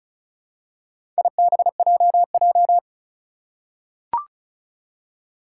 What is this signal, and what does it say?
Morse code, heard as a keyed tone, about 36 wpm, 700 Hz. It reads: IBJJ
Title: Four Challenging Letter-Number Combinations 35wpm